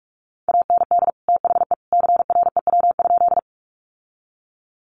Morse code, heard as a keyed tone, about 44 wpm, 700 Hz. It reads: AND THE CREW?
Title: Sentences from Top 1500 Words 45wpm